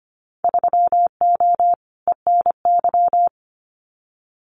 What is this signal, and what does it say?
Morse code, heard as a keyed tone, about 25 wpm, 700 Hz. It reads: 3O ENY
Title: Winter Field Day Exchanges Encore 25wpm